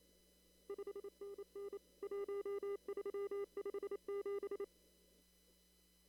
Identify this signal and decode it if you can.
Morse code, heard as a keyed tone, about 28 wpm, 400 Hz. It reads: 5NN 1357
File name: CW / kt5j wpx cw-op-k5pi